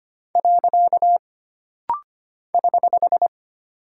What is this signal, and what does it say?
Morse code, heard as a keyed tone, about 25 wpm, 700 Hz. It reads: . <HH>
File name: Single Punctuation-Prosign - Rapid-Fire - 25wpm